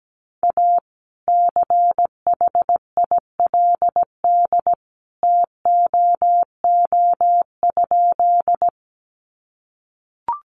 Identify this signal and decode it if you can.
Morse code, heard as a keyed tone, about 17 wpm, 700 Hz. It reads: A CHILD TOO?
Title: Sentences from Top 800 Words 17wpm